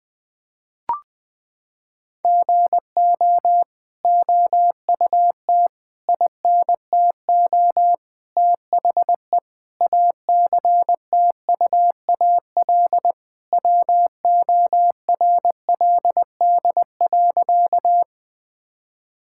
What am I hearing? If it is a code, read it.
Morse code, heard as a keyed tone, about 20 wpm, 700 Hz. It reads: GO OUT INTO THE ACTUAL WORLD.
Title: Sentences from Top 1000 Words - Review Part 5 - 20wpm